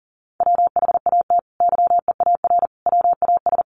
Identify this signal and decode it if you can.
Morse code, heard as a keyed tone, about 37 wpm, 700 Hz. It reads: WHAT YEAR WAS
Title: Sentences from Top 800 Words 40wpm